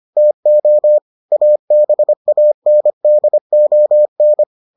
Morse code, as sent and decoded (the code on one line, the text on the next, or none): - --- / .- -... .- -. -.. --- -.
TO ABANDON